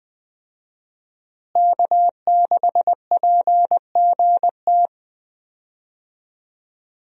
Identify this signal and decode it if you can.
Morse code, heard as a keyed tone, about 21 wpm, 700 Hz. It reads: K6PGT
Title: USA Call Signs - 5 Characters - Repeated 3x - 20wpm